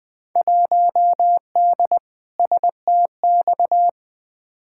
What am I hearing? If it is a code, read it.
Morse code, heard as a keyed tone, about 20 wpm, 700 Hz. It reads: 1D STX
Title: Field Day Exchanges - 20wpm